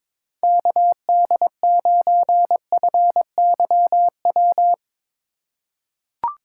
Morse code, heard as a keyed tone, about 22 wpm, 700 Hz. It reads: KD9FYW
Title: USA Call Signs - 2x Character Spacing - 22wpm